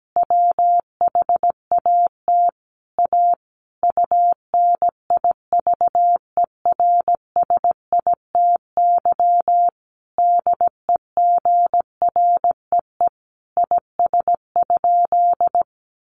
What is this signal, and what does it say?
Morse code, heard as a keyed tone, about 17 wpm, 700 Hz. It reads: WHAT A UNIVERSITY DEGREE IS?
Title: Sentences from Top 1500 Words 17wpm